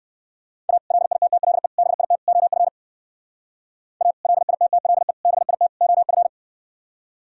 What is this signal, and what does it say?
Morse code, heard as a keyed tone, about 68 wpm, 700 Hz. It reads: A LITTLE BIT OF A LITTLE BIT OF